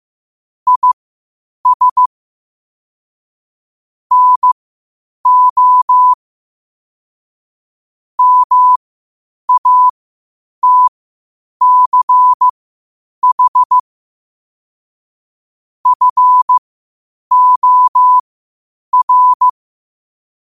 Morse code, heard as a keyed tone, about 15 wpm, spaced as for 8 wpm, 1000 Hz. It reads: IS NO MATCH FOR
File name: qotdfiles / 8wpm